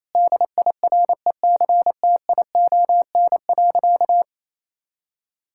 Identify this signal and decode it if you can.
Morse code, heard as a keyed tone, about 28 wpm, 700 Hz. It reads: DIRECTION.
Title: Sentences from Top 1000 Words - Review Part 5 - 28wpm